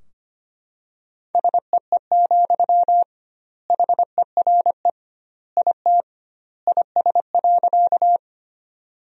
Morse code, heard as a keyed tone, about 25 wpm, 700 Hz. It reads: SEE, HERE IT IS.
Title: 2x Word-Spacing - Sentences from Top 200 Words - 25wpm